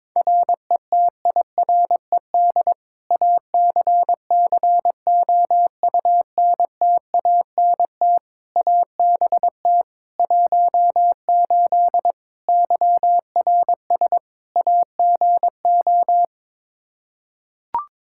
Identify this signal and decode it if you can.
Morse code, heard as a keyed tone, about 21 wpm, 700 Hz. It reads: RETIRED ACCOUNTANT ABT 18 YRS AGO
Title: K1USN SST QSO Practice 22wpm